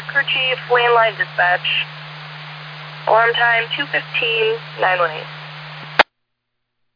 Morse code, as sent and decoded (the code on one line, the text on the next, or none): none